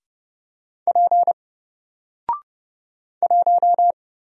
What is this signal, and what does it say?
Morse code, heard as a keyed tone, about 30 wpm, 700 Hz. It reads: P 1
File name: Single-Letter-Number-30wpm